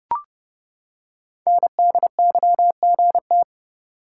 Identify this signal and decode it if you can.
Morse code, heard as a keyed tone, about 29 wpm, 700 Hz. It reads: NDYGT
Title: Code Groups - 5 Letter:Numbers - 30wpm